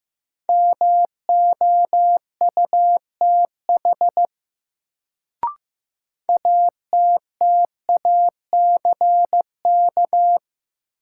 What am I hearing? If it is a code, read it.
Morse code, heard as a keyed tone, about 15 wpm, 700 Hz. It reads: MOUTH ATTACK
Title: Sentences from Top 1500 Words - New Word Drill Alt - 15wpm